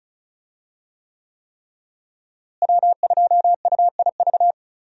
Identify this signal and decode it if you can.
Morse code, heard as a keyed tone, about 36 wpm, 700 Hz. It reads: W2UIV
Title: USA Call Signs - 5 Characters - Repeated 3x - 35wpm